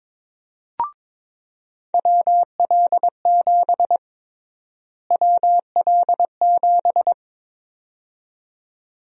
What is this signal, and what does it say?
Morse code, heard as a keyed tone, about 22 wpm, 700 Hz. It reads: WL7 WL7
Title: USA Call Sign Prefixes - Repeated 2x - 22wpm